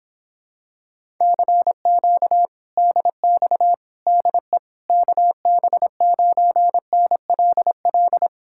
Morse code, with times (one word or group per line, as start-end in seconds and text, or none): 1.2-2.5 CQ
2.8-3.7 DX
4.1-4.6 DE
4.9-8.3 KB9NLL